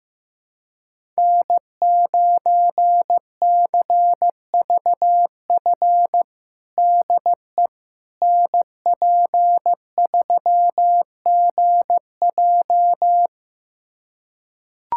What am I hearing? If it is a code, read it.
Morse code, heard as a keyed tone, about 15 wpm, 700 Hz. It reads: N9CVF DE NP3GJ
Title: USA Callsign DE Callsign - 5 Characters - 15wpm